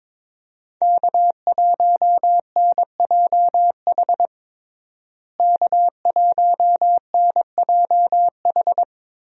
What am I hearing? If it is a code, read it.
Morse code, heard as a keyed tone, about 21 wpm, 700 Hz. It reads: K1NJH K1NJH
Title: USA Call Signs - 5 Characters - Repeated 3x - 22wpm